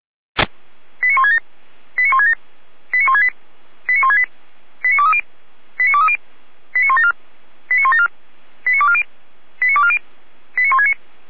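Five-tone selective call (ZVEI), heard as quick runs of five tones, tones of about 70 ms. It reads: 89167 89167 89168 89168 89230 89230 89174 89174 89240 89240 89169
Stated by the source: ZVEI five-tone call received by SDR, logged on server of feuerwehr-orlamuende.org (Feuerwehr Orlamünde).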